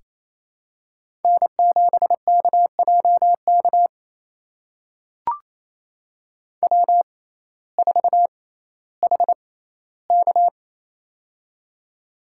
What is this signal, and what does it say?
Morse code, heard as a keyed tone, about 28 wpm, spaced as for 15 wpm, 700 Hz. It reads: N7KJK W4HK
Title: USA Call Signs - 6x Character Spacing - 28wpm